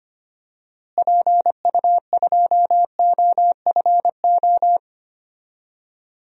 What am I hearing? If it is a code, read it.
Morse code, heard as a keyed tone, about 25 wpm, 700 Hz. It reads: PU2OFO
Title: DXCC-100 Call Signs - 6 Characters - Repeated 3x - 25wpm